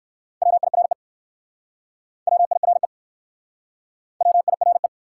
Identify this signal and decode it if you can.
Morse code, heard as a keyed tone, about 62 wpm, 700 Hz. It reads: WIRE WIRE WIRE